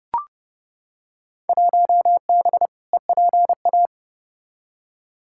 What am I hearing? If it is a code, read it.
Morse code, heard as a keyed tone, about 30 wpm, 700 Hz. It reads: 1B EPA
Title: Field Day Exchanges Encore - 30wpm